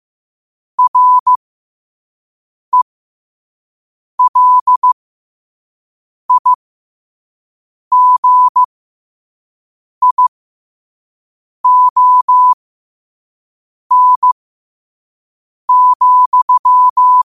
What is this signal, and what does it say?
Morse code, heard as a keyed tone, about 15 wpm, spaced as for 5 wpm, 1000 Hz. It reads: RELIGION,
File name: qotdfiles / 5wpm